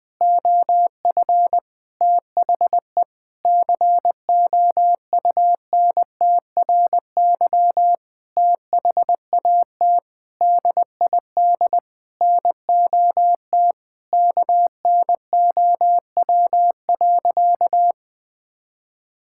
Morse code, heard as a keyed tone, about 19 wpm, 700 Hz. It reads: OF THE COUNTRY THAT DID NOT KNOW.
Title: Sentences from Top 1600 Words 20wpm